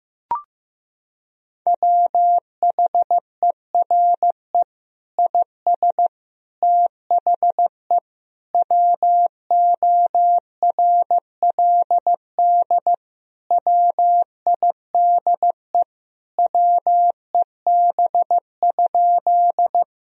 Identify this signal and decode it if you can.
Morse code, heard as a keyed tone, about 15 wpm, 700 Hz. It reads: WHERE IS THE WORLD WIDE WEB?
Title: Sentences from Top 1000 Words - Review Part 4 - 15wpm